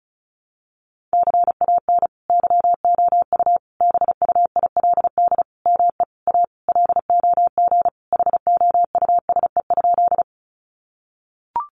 Morse code, heard as a keyed tone, about 34 wpm, 700 Hz. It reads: CAN YOU BUILD ME A LOG HOUSE?
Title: Sentences from Top 1400 Words 35wpm